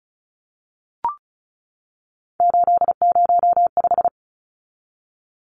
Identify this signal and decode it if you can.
Morse code, heard as a keyed tone, about 34 wpm, 700 Hz. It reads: 805
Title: Numbers - 3-Digit 35wpm